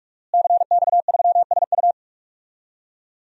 Morse code, heard as a keyed tone, about 41 wpm, 700 Hz. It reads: CX3SU
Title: DXCC-100 Call Signs - 5 Characters - Repeated 3x - 45wpm